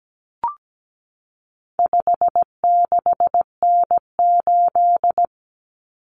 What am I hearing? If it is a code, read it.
Morse code, heard as a keyed tone, about 17 wpm, 700 Hz. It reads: 56N8
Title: Common Cut Numbers - 4-Digit 17wpm